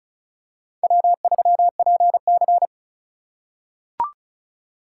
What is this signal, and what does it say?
Morse code, heard as a keyed tone, about 36 wpm, 700 Hz. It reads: W3PC
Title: USA Call Signs - 4 Characters - 2x Character Spacing - 35wpm